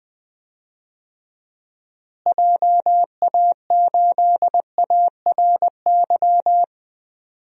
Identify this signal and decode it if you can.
Morse code, heard as a keyed tone, about 20 wpm, 700 Hz. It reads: JA8ARY
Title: DXCC-100 Call Signs - 6 Characters - Repeated 2x - 20wpm